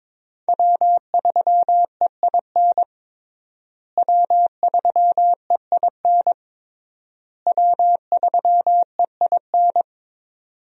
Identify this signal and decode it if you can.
Morse code, heard as a keyed tone, about 22 wpm, 700 Hz. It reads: W3EIN W3EIN W3EIN
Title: USA Call Signs - 5 Characters - Repeated 3x - 22wpm